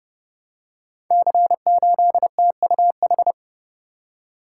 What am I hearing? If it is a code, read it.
Morse code, heard as a keyed tone, about 28 wpm, 700 Hz. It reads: C8TUH
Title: Code Groups - 5 Letter:Numbers - 30wpm